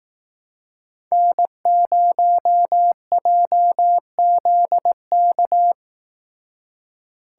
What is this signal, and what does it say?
Morse code, heard as a keyed tone, about 18 wpm, 700 Hz. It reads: N0JZK
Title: Speed Racing - USA 1x3 Call Signs 15wpm